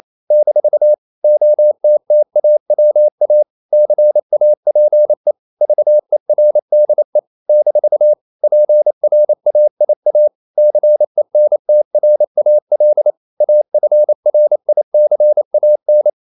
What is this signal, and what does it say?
Morse code, heard as a keyed tone, about 28 wpm, 600 Hz. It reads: - OTTAWA CAPE VERDE - PRAIA CENTRAL AFRICAN
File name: CW / Practice / NationalCapitals28